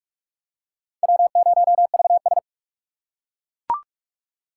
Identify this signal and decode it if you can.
Morse code, heard as a keyed tone, about 38 wpm, 700 Hz. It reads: W0VS